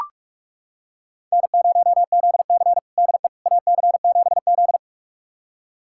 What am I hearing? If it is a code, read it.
Morse code, heard as a keyed tone, about 42 wpm, 700 Hz. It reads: N0ZC DE AC7Z
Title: USA Callsign DE Callsign - 4 Characters - 45wpm